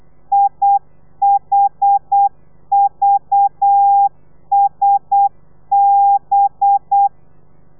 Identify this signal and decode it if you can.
Morse code, heard as a keyed tone, about 8 wpm, 800 Hz. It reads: IHVSB
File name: telegrafia / i2tfj / Lez4 35